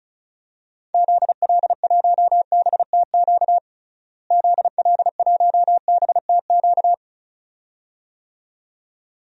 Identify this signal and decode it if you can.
Morse code, heard as a keyed tone, about 36 wpm, 700 Hz. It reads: ZL1BTQ ZL1BTQ
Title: DXCC-100 Call Signs - 6 Characters - Repeated 2x - 35wpm